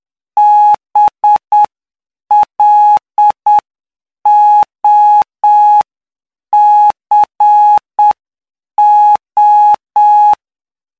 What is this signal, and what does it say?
Morse code, heard as a keyed tone, about 7 wpm, 840 Hz. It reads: BLOCO
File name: cw / py2lcd / 16-PY2LCD-Li��o 25